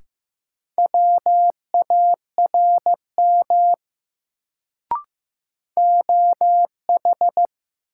Morse code, heard as a keyed tone, about 15 wpm, 700 Hz. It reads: WARM OH
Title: Sentences from Top 500 Words - New Word Drill - 15wpm